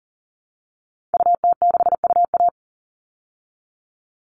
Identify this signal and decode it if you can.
Morse code, heard as a keyed tone, about 40 wpm, 700 Hz. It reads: UT6UA
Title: Call Signs Repeated 3x - 40wpm